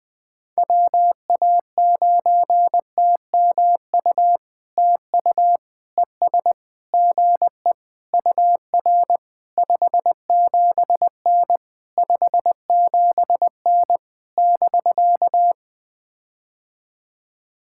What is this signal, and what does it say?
Morse code, heard as a keyed tone, about 20 wpm, 700 Hz. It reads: WA9TMU TU ES GE UR 57N 57N <BK>
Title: POTA QSO Practice - 20wpm